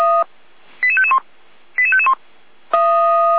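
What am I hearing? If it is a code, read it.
Five-tone selective call (ZVEI), heard as quick runs of five tones, tones of about 70 ms. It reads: 88551 88551
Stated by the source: ZVEI five-tone call received via SDR.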